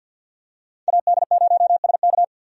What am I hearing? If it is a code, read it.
Morse code, heard as a keyed tone, about 49 wpm, 700 Hz. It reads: AD0SK
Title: USA Call Signs - 5 Characters - 50wpm